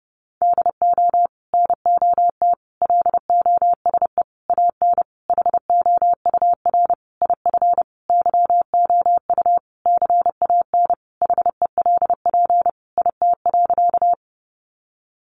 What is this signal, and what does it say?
Morse code, heard as a keyed tone, about 30 wpm, 700 Hz. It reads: DO NOT LOSE AN HOUR IF YOU CAN HELP IT.
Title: Sentences from Top 1100 Words 30wpm